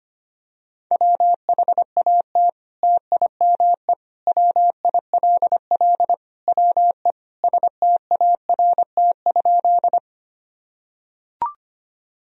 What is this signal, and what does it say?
Morse code, heard as a keyed tone, about 25 wpm, 700 Hz. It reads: WHAT TIME WILL WE START?
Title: Sentences from Top 1000 Words - Review Part 1 - 25wpm